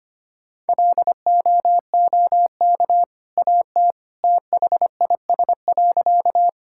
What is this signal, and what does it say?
Morse code, heard as a keyed tone, about 25 wpm, 700 Hz. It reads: LOOK AT THIS.
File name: practice / 25wpm / Sentences with 2-Word Phrases - 25wpm